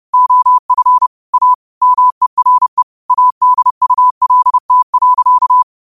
Morse code, heard as a keyed tone, about 30 wpm, 1000 Hz. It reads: OF A MERE ADULT.
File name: qotdfiles / 30wpm